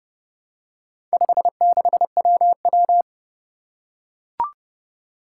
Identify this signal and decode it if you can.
Morse code, heard as a keyed tone, about 30 wpm, 700 Hz. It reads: 56WW
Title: Four Challenging Letter-Number Combinations 30wpm